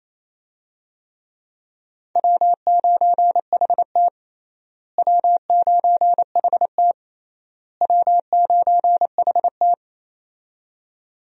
Morse code, heard as a keyed tone, about 28 wpm, 700 Hz. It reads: W9HT W9HT W9HT
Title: USA Call Signs - 4 Characters - Repeated 3x - 28wpm